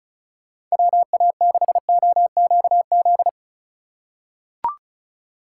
Morse code, heard as a keyed tone, about 36 wpm, 700 Hz. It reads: WA6OQZ
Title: USA Call Signs - 6 Characters - 2x Character Spacing - 35wpm